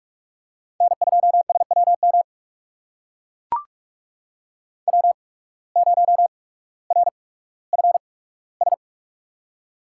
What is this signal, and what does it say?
Morse code, heard as a keyed tone, about 44 wpm, spaced as for 19 wpm, 700 Hz. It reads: N2SWM W0RFS
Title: USA Call Signs - 5 Characters - 8x Character Spacing - 45wpm